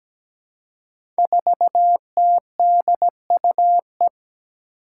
Morse code, heard as a keyed tone, about 17 wpm, 700 Hz. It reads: 4TDUE
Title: Five Challenging Letter-Number Combinations 17wpm